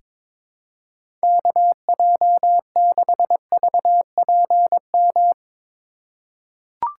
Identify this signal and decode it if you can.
Morse code, heard as a keyed tone, about 21 wpm, 700 Hz. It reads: KJ6VPM
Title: USA Call Signs - 6 Characters - 4x Character Spacing - 22wpm